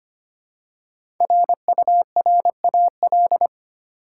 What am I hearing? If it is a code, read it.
Morse code, heard as a keyed tone, about 25 wpm, 700 Hz. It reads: RURAL
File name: 5 Letter Words - First 3 Letters Given - 25wpm